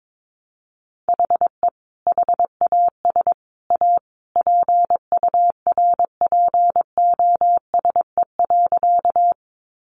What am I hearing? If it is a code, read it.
Morse code, heard as a keyed tone, about 22 wpm, 700 Hz. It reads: HE HAS A PURPOSE.
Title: Sentences from Top 800 Words 22wpm